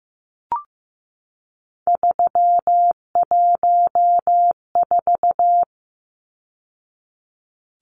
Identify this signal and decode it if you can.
Morse code, heard as a keyed tone, about 15 wpm, 700 Hz. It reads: 314
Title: Numbers - 3-Digit 15wpm